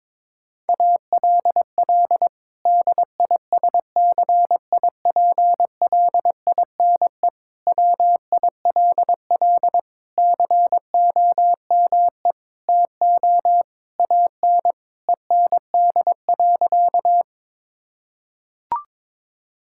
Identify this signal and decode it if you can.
Morse code, heard as a keyed tone, about 22 wpm, 700 Hz. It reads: ALL DISCIPLINE WILL COME TO AN END.
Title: Sentences from Top 1000 Words - Review Part 6 - 22wpm